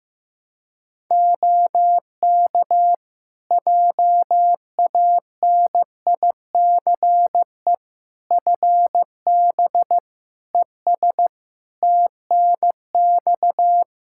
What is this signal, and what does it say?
Morse code, heard as a keyed tone, about 16 wpm, 700 Hz. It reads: OK JANICE FB ES TNX
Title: K1USN SST QSO Practice 15wpm